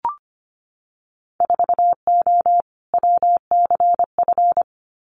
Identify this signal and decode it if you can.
Morse code, heard as a keyed tone, about 25 wpm, 700 Hz. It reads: 4O WCF
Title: Winter Field Day Exchanges Encore 25wpm